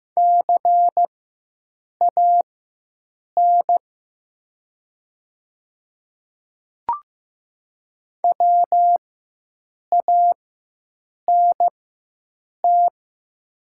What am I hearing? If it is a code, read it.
Morse code, heard as a keyed tone, about 15 wpm, spaced as for 7 wpm, 700 Hz. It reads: CAN WANT
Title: 4x Character Spacing - Top 300 Words - 15wpm